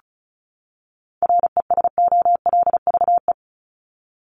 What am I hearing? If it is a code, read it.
Morse code, heard as a keyed tone, about 34 wpm, 700 Hz. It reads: RESOLVE
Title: Top 1751-2000 Words 35wpm